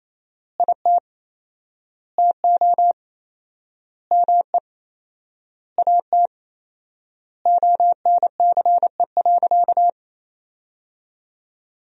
Morse code, heard as a keyed tone, about 28 wpm, 700 Hz. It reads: IT TO ME AT ONCE.
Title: 4x Word-Spacing - Sentences from Top 500 Words - 28wpm